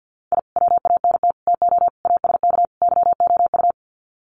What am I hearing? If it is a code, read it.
Morse code, heard as a keyed tone, about 56 wpm, 700 Hz. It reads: I WANT TO ASK YOU
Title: Sets of 5 Words 50wpm